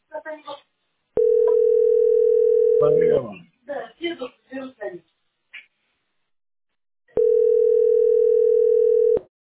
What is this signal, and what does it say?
Ringback tone (North America), 440 + 480 Hz, with speech.
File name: 893f47c0-11a8-4cf1-ab3f